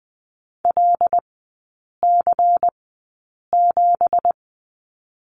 Morse code, heard as a keyed tone, about 20 wpm, spaced as for 8 wpm, 700 Hz. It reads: LC7
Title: 166 Lesson 27e - Number 7 - Three Characters 20wpm